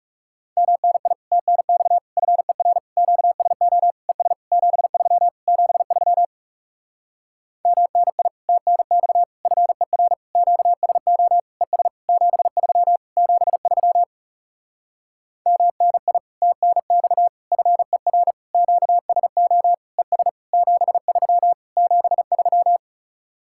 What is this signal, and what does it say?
Morse code, heard as a keyed tone, about 40 wpm, 700 Hz. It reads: MNI TNX FER QSO ES 73 73 MNI TNX FER QSO ES 73 73 MNI TNX FER QSO ES 73 73